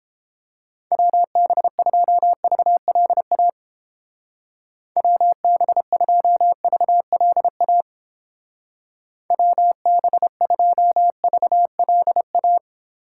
Morse code, heard as a keyed tone, about 29 wpm, 700 Hz. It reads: WB2VLA WB2VLA WB2VLA